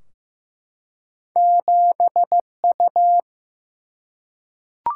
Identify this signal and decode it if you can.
Morse code, heard as a keyed tone, about 15 wpm, 700 Hz. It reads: 7U
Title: Single-Letter Single-Number 15wpm